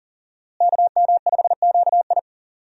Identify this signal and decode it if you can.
Morse code, heard as a keyed tone, about 40 wpm, 700 Hz. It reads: KM5QI
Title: USA Call Signs - 5 Characters - Repeated 3x - 40wpm